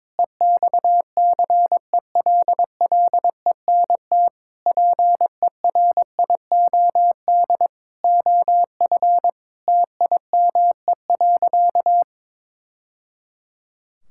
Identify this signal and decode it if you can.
Morse code, heard as a keyed tone, about 22 wpm, 700 Hz. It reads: EXCELLENT PERIOD OF TIME.
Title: Sentences from Top 1000 Words - Review Part 5 - 22wpm